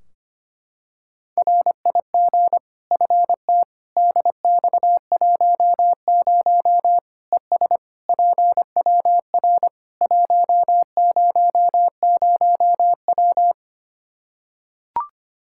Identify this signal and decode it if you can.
Morse code, heard as a keyed tone, about 25 wpm, 700 Hz. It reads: RIG FT DX10 ES PWR 100W
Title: K1USN SST QSO Practice 25wpm